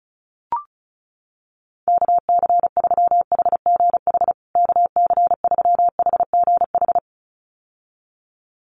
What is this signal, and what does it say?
Morse code, heard as a keyed tone, about 34 wpm, 700 Hz. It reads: KC3HGH KC3HGH
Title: USA Call Signs Repeated 2x 35wpm